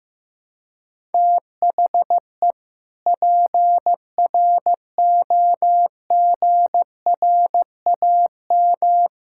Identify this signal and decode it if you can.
Morse code, heard as a keyed tone, about 15 wpm, 700 Hz. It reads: THE PROGRAM
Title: Sentences from Top 1000 Words - Review Part 5 - 15wpm